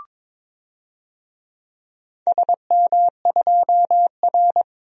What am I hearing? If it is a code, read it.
Morse code, heard as a keyed tone, about 22 wpm, 700 Hz. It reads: SM2R
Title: DXCC-100 Call Signs - 4 Characters - 22wpm